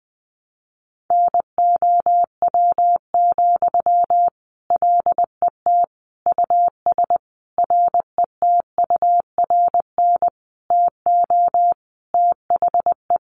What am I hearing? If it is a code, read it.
Morse code, heard as a keyed tone, about 20 wpm, 700 Hz. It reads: NOW, LET US RETURN TO THE
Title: Sentences from Top 800 Words 20wpm